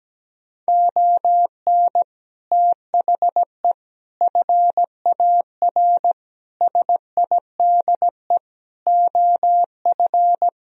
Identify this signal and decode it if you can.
Morse code, heard as a keyed tone, about 17 wpm, 700 Hz. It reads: ON THE FAR SIDE OF
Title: Speed-Racing - Sets of 5 Words 17wpm